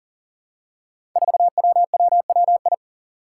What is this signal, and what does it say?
Morse code, heard as a keyed tone, about 39 wpm, 700 Hz. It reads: 4WWWI
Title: Five Challenging Letter-Number Combinations 40wpm